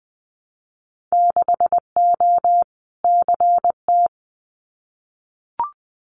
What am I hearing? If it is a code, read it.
Morse code, heard as a keyed tone, about 20 wpm, 700 Hz. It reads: 6O CT
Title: Winter Field Day Exchanges Encore 20wpm